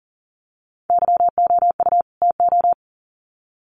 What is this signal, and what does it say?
Morse code, heard as a keyed tone, about 40 wpm, 700 Hz. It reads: YOU TO
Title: Sets of 2 Words 40wpm